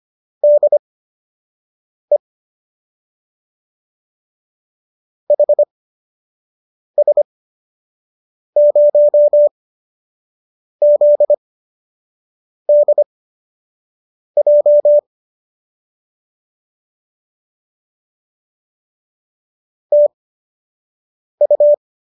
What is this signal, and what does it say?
Morse code, heard as a keyed tone, about 25 wpm, spaced as for 6 wpm, 590 Hz. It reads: DE HS0ZDJ TU